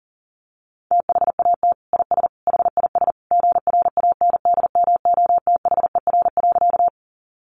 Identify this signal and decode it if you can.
Morse code, heard as a keyed tone, about 40 wpm, 700 Hz. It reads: THAT IS HIS GRANDMOTHER.
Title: Sentences from Top 1200 Words 40wpm